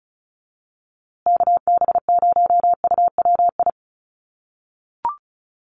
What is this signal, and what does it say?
Morse code, heard as a keyed tone, about 34 wpm, 700 Hz. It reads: KB0UWI
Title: USA Call Signs Repeated 2x 35wpm